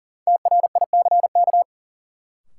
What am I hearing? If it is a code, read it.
Morse code, heard as a keyed tone, about 40 wpm, 700 Hz. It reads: TRICK
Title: Sentences from Top 1500 Words - New Word Drill - 40wpm